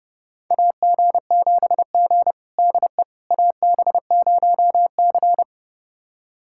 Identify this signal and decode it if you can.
Morse code, heard as a keyed tone, about 30 wpm, 700 Hz. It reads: AG7G DE AB0C
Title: USA Callsign DE Callsign - 4 Characters - 30wpm